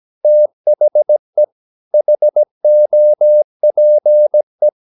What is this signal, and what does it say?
Morse code, heard as a keyed tone, about 17 wpm, 600 Hz. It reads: THE HOPE
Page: Romanen Never af Ken Follet på engelsk.